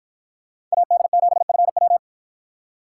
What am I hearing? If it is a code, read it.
Morse code, heard as a keyed tone, about 53 wpm, 700 Hz. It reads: AD7FW